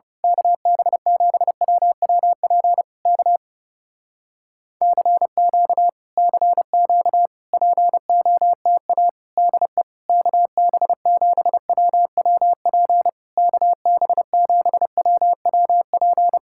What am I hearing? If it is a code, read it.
Morse code, heard as a keyed tone, about 30 wpm, 700 Hz. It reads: KB7WWP K CQ CQ POTA DE KB7WWP KB7WWP
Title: Speed-Racing Common QSO Phrases Encore - 30wpm